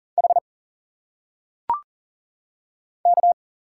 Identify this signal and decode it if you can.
Morse code, heard as a keyed tone, about 38 wpm, 700 Hz. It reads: H K
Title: Single Letters - Phonetic Pronunciation 40wpm